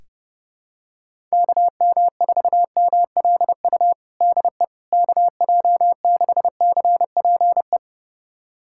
Morse code, heard as a keyed tone, about 30 wpm, 700 Hz. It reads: KM4MLU DE KJ6CPE
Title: USA Callsign DE Callsign 30wpm